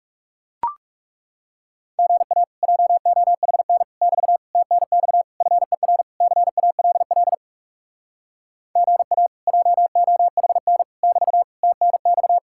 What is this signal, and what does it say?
Morse code, heard as a keyed tone, about 40 wpm, 700 Hz. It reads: GA JOHN <BT> TNX FER CALL GA JOHN <BT> TNX